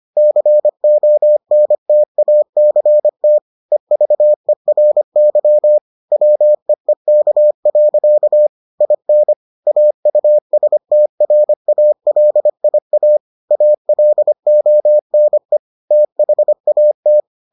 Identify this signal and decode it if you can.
Morse code, heard as a keyed tone, about 25 wpm, 600 Hz. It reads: CONTACT EVERY WEEK. IN AUSTRALIA ALONE THAT